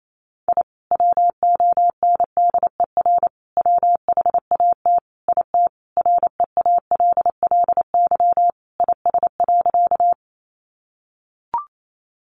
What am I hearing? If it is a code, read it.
Morse code, heard as a keyed tone, about 28 wpm, 700 Hz. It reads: I WONDER WHAT IT REALLY IS.
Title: Sentences from Top 600 Words 28wpm